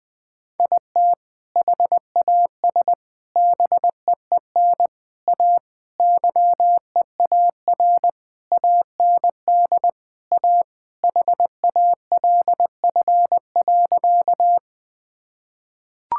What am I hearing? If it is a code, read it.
Morse code, heard as a keyed tone, about 21 wpm, 700 Hz. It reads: IT HAS BEEN A YEAR AND A HALF.
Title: Sentences with 5-Word Phrases - 20wpm